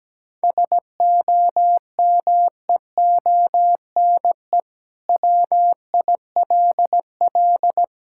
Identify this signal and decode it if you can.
Morse code, heard as a keyed tone, about 17 wpm, 700 Hz. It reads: SOMEONE WILL
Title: Sentences from Top 1000 Words - Review Part 6 - 17wpm